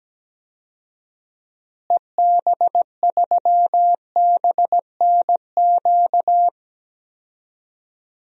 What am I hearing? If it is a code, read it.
Morse code, heard as a keyed tone, about 17 wpm, 700 Hz. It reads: EB3BNQ
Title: DXCC-100 Call Signs - 6 Characters - Repeated 2x - 17wpm